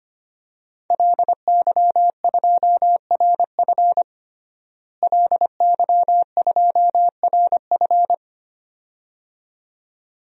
Morse code, heard as a keyed tone, about 25 wpm, 700 Hz. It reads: LY2RF LY2RF
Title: DXCC-100 Call Signs - 5 Characters - Repeated 2x - 25wpm